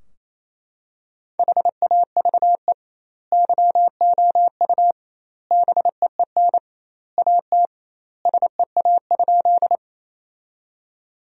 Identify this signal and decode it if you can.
Morse code, heard as a keyed tone, about 28 wpm, 700 Hz. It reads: HAVE YOU BEEN AT SEA?
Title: 2x Word-Spacing - Sentences from Top 300 Words - 28wpm